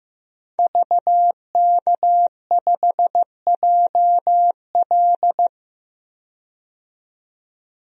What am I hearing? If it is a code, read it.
Morse code, heard as a keyed tone, about 15 wpm, 700 Hz. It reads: VK5JL
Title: DXCC-100 Call Signs - 5 Characters - 15wpm